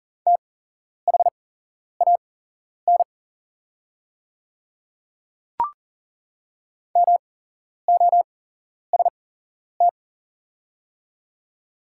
Morse code, heard as a keyed tone, about 40 wpm, spaced as for 11 wpm, 700 Hz. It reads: THAN MOST
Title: 8x Character Spacing - Top 100 Words - 40wpm